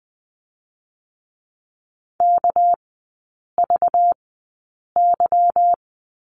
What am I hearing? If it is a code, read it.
Morse code, heard as a keyed tone, about 20 wpm, spaced as for 8 wpm, 700 Hz. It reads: KVY
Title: 198 Lesson 32e - Letter k - Three Characters 20wpm